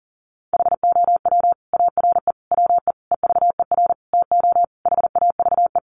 Morse code, heard as a keyed tone, about 40 wpm, 700 Hz. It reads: HOW ARE WE EVER TO HAVE
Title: Sentences from Top 1500 Words 40wpm